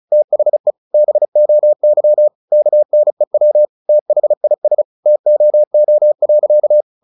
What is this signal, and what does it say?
Morse code, heard as a keyed tone, about 34 wpm, 600 Hz. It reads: THE BOY KNEW THIS TOO.
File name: the old man and the sea / the old man and the sea 1 35wpm